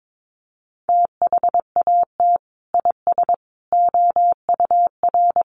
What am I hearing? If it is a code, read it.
Morse code, heard as a keyed tone, about 22 wpm, 700 Hz. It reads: THAT IS OUR
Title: Sentences from Top 1400 Words 22wpm